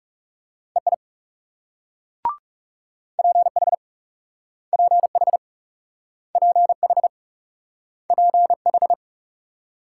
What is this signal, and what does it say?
Morse code, heard as a keyed tone, about 39 wpm, 700 Hz. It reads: EI PH PH PH PH